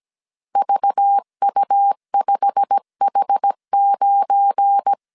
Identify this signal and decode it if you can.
Morse code, heard as a keyed tone, about 17 wpm, 800 Hz. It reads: VU5H9
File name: CW / DET / 1 9 1500C S C 16wpm 2909